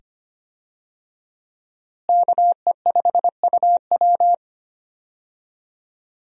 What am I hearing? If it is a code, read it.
Morse code, heard as a keyed tone, about 25 wpm, 700 Hz. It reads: KE5UW
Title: USA Call Signs - 5 Characters - Repeated 2x - 25wpm